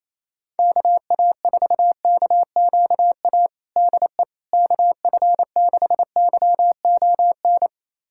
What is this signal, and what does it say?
Morse code, heard as a keyed tone, about 28 wpm, 700 Hz. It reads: KA4KQA DE KF6YON
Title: USA Callsign DE Callsign 28wpm